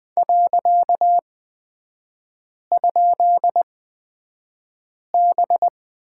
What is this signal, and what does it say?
Morse code, heard as a keyed tone, about 20 wpm, spaced as for 5 wpm, 700 Hz. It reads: .?B
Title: 255 Lesson 41e - Period - Three Characters 20wpm